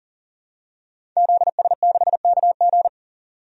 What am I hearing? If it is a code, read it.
Morse code, heard as a keyed tone, about 40 wpm, 700 Hz. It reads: ZS6KG
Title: DXCC-100 Call Signs - 5 Characters - 40wpm